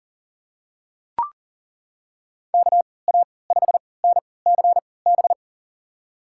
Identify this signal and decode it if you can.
Morse code, heard as a keyed tone, about 40 wpm, spaced as for 23 wpm, 700 Hz. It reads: KA5NCB
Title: USA Call Signs - 6 Characters - 3x Character Spacing - 40wpm